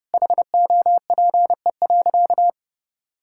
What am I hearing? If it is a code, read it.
Morse code, heard as a keyed tone, about 31 wpm, 700 Hz. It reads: HOPE.
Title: Sentences from Top 1000 Words - Review Part 5 - 30wpm